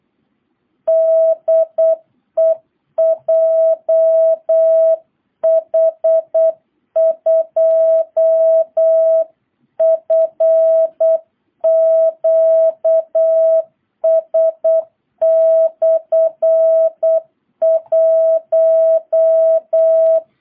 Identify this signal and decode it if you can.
Morse code, heard as a keyed tone, about 8 wpm, 660 Hz. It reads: DE JH2FQS/1